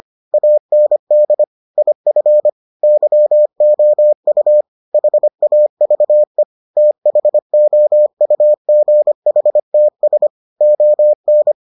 Morse code, heard as a keyed tone, about 25 wpm, 600 Hz. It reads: AND IF YOU HAVE THOUGHTS ON